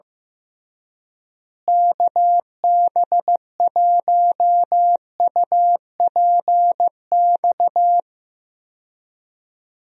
Morse code, heard as a keyed tone, about 15 wpm, 700 Hz. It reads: KB1UPX
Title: USA Call Signs - 6 Characters - Repeated 3x - 15wpm